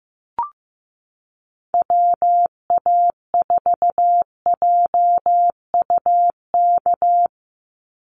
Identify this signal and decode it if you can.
Morse code, heard as a keyed tone, about 14 wpm, 700 Hz. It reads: WA4JUK
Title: Call Signs Repeated 3x - 15wpm